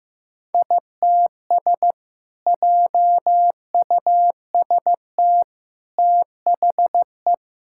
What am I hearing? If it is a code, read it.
Morse code, heard as a keyed tone, about 15 wpm, 700 Hz. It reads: ITS JUST THE
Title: Sentences from Top 1000 Words - Review Part 5 - 15wpm